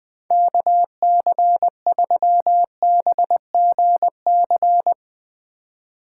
Morse code, heard as a keyed tone, about 20 wpm, 700 Hz. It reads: KC3BGC